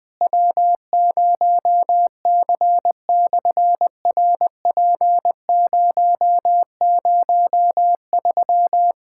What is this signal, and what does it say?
Morse code, heard as a keyed tone, about 20 wpm, 700 Hz. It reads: W0C/RP003
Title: SOTA QSO Practice 20wpm